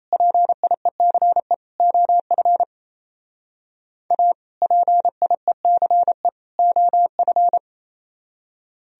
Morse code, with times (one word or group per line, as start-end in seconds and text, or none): 0.1-1.5 PIECE
1.8-2.6 OF
4.1-4.3 A
4.6-6.3 PIECE
6.6-7.6 OF